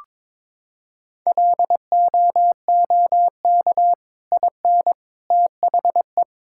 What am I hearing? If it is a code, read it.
Morse code, heard as a keyed tone, about 22 wpm, 700 Hz. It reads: LOOK IN THE
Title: Sentences from Top 1600 Words 22wpm